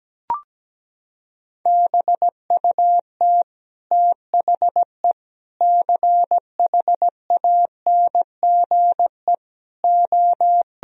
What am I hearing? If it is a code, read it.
Morse code, heard as a keyed tone, about 17 wpm, 700 Hz. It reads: BUT THE CHANGE O
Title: Sentences from Top 1000 Words - Review Part 4 - 17wpm